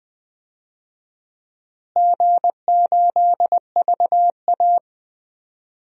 Morse code, heard as a keyed tone, about 20 wpm, 700 Hz. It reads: G8VA
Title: DXCC-100 Call Signs - 4 Characters - Repeated 3x - 20wpm